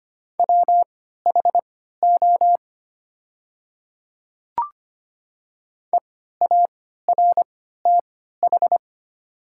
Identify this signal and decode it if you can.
Morse code, heard as a keyed tone, about 25 wpm, spaced as for 14 wpm, 690 Hz. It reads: WHO EARTH
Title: 3x Character Spacing - Top 200 Words - 25wpm